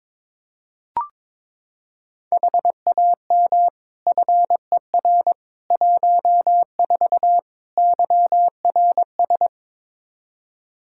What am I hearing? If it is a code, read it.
Morse code, heard as a keyed tone, about 22 wpm, 700 Hz. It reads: HAM FER 14 YRS
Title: K1USN SST QSO Practice 22wpm